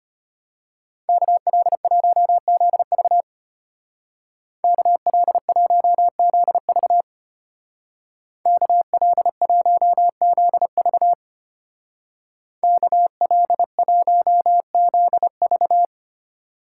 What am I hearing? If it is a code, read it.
Morse code, heard as a keyed tone, about 30 wpm, 700 Hz. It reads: KL1ZV KL1ZV KL1ZV KL1ZV
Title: Speed Racing - USA 2x2 Call Signs 25wpm